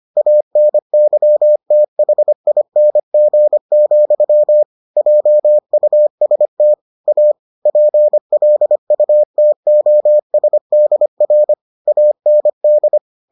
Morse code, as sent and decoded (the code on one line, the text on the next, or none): .- -. -.-- - .... .. -. --. --..-- / .--- ..- ... - / .- / .--. .-.. ..- - --- ... -.. .-. / .- -. -..
ANYTHING, JUST A PLUTOSDR AND